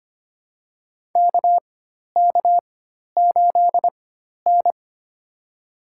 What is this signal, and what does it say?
Morse code, heard as a keyed tone, about 25 wpm, spaced as for 12 wpm, 700 Hz. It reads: KK8N
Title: USA Call Signs - 4 Characters - 4x Character Spacing - 25wpm